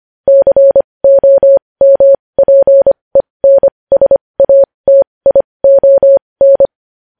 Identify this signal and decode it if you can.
Morse code, heard as a keyed tone, about 25 wpm, spaced as for 20 wpm, 550 Hz. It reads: COMPENSATION
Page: news-headlines-in-morse-code-at-20-wpm